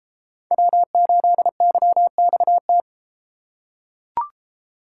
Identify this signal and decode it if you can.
Morse code, heard as a keyed tone, about 30 wpm, 700 Hz. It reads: W8YXT